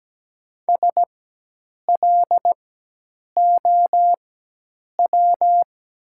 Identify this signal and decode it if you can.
Morse code, heard as a keyed tone, about 17 wpm, spaced as for 8 wpm, 700 Hz. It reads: SLOW
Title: 4x Character Spacing - Top 500 Words - 17wpm